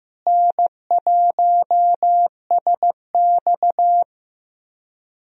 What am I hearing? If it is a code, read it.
Morse code, heard as a keyed tone, about 15 wpm, 700 Hz. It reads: N1SX
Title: USA Call Signs - 4 Characters - Repeated 3x - 15wpm